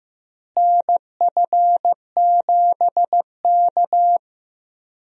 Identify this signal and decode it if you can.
Morse code, heard as a keyed tone, about 15 wpm, 700 Hz. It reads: NF7K
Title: USA Call Signs - 4 Characters - Repeated 2x - 15wpm